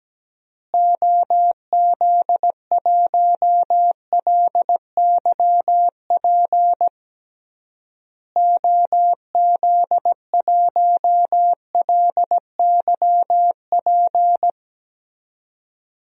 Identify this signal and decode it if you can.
Morse code, heard as a keyed tone, about 17 wpm, 700 Hz. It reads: OZ1LYP OZ1LYP
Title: DXCC-100 Call Signs - 6 Characters - Repeated 3x - 17wpm